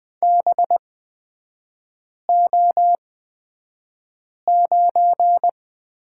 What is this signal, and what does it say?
Morse code, heard as a keyed tone, about 20 wpm, spaced as for 5 wpm, 700 Hz. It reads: BO9
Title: 224 Lesson 36e - BT - Three Characters 20wpm